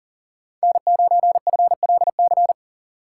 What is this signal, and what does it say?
Morse code, heard as a keyed tone, about 40 wpm, 700 Hz. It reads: N9FLC
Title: USA Call Signs - 2x Character Spacing - 40wpm